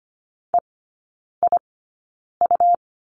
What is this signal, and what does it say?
Morse code, heard as a keyed tone, about 25 wpm, spaced as for 9 wpm, 700 Hz. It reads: EIU
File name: Lesson 17e - Letter u - Three Characters 25wpm